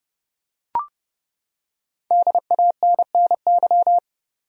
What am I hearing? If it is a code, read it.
Morse code, heard as a keyed tone, about 30 wpm, 700 Hz. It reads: DANNY
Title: USA Common Names - 30wpm